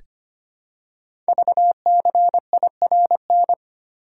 Morse code, heard as a keyed tone, about 25 wpm, 700 Hz. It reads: VCIRN
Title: Code Groups - 5 Letter:Numbers - 25wpm